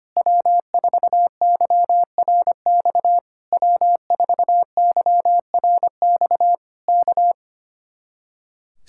Morse code, heard as a keyed tone, about 25 wpm, 700 Hz. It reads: W4YRX W4YRX K
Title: Speed-Racing Common QSO Phrases - 25wpm